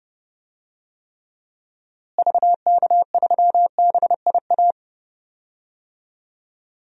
Morse code, heard as a keyed tone, about 30 wpm, 700 Hz. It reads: VK3BIA
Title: DXCC-100 Call Signs - 6 Characters - Repeated 2x - 30wpm